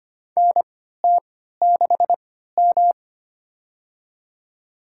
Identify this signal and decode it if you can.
Morse code, heard as a keyed tone, about 25 wpm, spaced as for 14 wpm, 700 Hz. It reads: NT6M
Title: USA Call Signs - 3x Character Spacing - 25wpm